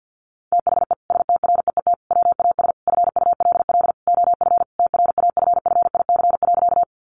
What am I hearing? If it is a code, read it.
Morse code, heard as a keyed tone, about 50 wpm, 700 Hz. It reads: THE STREET WAS FULL OF TRAFFIC.
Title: Sentences from Top 1300 Words 50wpm